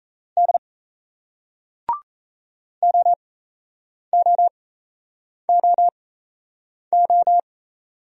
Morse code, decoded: D OOOO

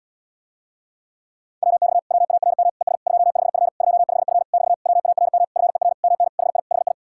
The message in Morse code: WX WARM ES 25F 25F <BT> WARM FER MN HI HI

